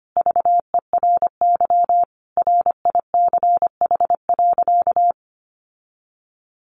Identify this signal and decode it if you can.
Morse code, heard as a keyed tone, about 25 wpm, 700 Hz. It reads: VERY RICH.
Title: Sentences from Top 1200 Words 25wpm